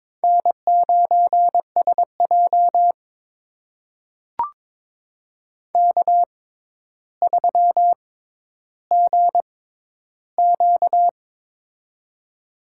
Morse code, heard as a keyed tone, about 22 wpm, spaced as for 11 wpm, 700 Hz. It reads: N9SJ K3GQ
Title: USA Call Signs - 4 Characters - 6x Character Spacing - 22wpm